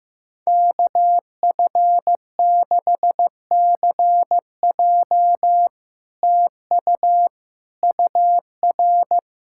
Morse code, heard as a keyed tone, about 15 wpm, 700 Hz. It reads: KF6CJ TU UR
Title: SOTA QSO Practice 15wpm